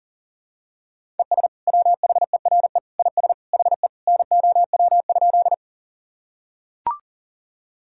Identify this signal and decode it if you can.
Morse code, heard as a keyed tone, about 37 wpm, 700 Hz. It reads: ES WHERE IS HE NOW?
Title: Sentences from Top 200 Words and Abbreviations 40wpm